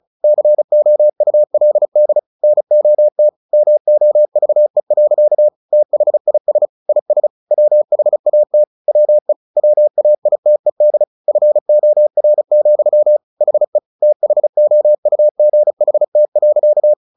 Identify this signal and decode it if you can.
Morse code, heard as a keyed tone, about 34 wpm, 600 Hz. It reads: COULD NOT MOVE. THIS IS WHAT WE WAITED FOR, HE THOUGHT.
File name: the old man and the sea 2 35wpm